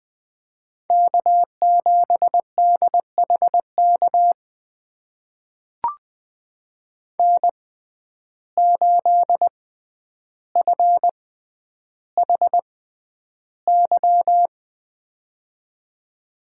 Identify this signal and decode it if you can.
Morse code, heard as a keyed tone, about 20 wpm, spaced as for 10 wpm, 700 Hz. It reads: K7DHK N8FHY
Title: USA Call Signs - 5 Characters - 6x Character Spacing - 20wpm